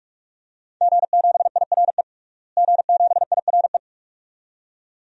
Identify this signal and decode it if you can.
Morse code, heard as a keyed tone, about 43 wpm, 700 Hz. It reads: G7IRE G7IRE
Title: DXCC-100 Call Signs - 5 Characters - Repeated 2x - 45wpm